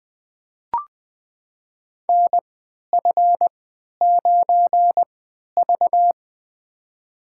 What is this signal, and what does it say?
Morse code, heard as a keyed tone, about 20 wpm, spaced as for 11 wpm, 700 Hz. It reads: NF9V
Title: USA Call Signs - 4 Characters - 3x Character Spacing - 20wpm